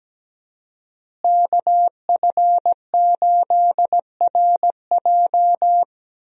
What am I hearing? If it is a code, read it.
Morse code, heard as a keyed tone, about 17 wpm, 700 Hz. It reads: KF8RJ
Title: USA Call Signs - 5 Characters - Repeated 3x - 17wpm